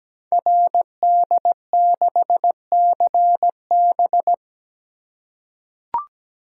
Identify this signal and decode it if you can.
Morse code, heard as a keyed tone, about 17 wpm, 700 Hz. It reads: RD6CB
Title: Code Groups - 5 Letter:Numbers - 17wpm